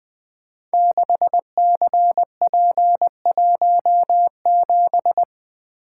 Morse code, heard as a keyed tone, about 20 wpm, 700 Hz. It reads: 6CP17
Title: Code Groups - 5 Letter:Numbers - 20wpm